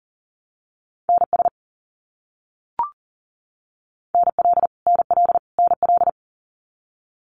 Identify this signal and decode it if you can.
Morse code, heard as a keyed tone, about 40 wpm, 700 Hz. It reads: NS NL NL NL
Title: Canadian Provinces and Territories - Repeated 3x - 40wpm